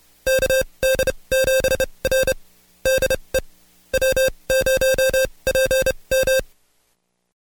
Morse code, heard as a keyed tone, about 29 wpm, spaced as for 23 wpm, 530 Hz. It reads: KD7R DE W0PM
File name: Code Quick / AudioDownld2 / Seg138-Stepper11 23WPM